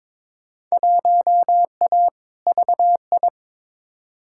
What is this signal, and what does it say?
Morse code, heard as a keyed tone, about 22 wpm, 700 Hz. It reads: 1A VI
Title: Field Day Exchanges Encore - 22wpm